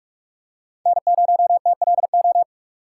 Morse code, heard as a keyed tone, about 46 wpm, 700 Hz. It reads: N0TLO
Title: USA Call Signs - 2x Character Spacing - 45wpm